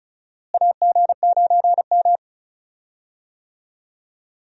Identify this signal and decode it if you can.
Morse code, heard as a keyed tone, about 36 wpm, 700 Hz. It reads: AG9M
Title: USA Call Signs - 4 Characters - Repeated 2x - 35wpm